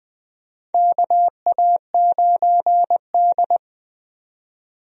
Morse code, heard as a keyed tone, about 20 wpm, 700 Hz. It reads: KA9D
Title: USA Call Signs - 4 Characters - 20wpm